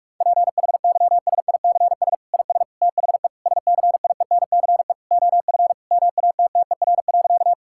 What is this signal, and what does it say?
Morse code, heard as a keyed tone, about 46 wpm, 700 Hz. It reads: PHYSICS IS THE SCIENCE OF MATTER.
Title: Sentences from Top 1000 Words - Review Part 5 - 45wpm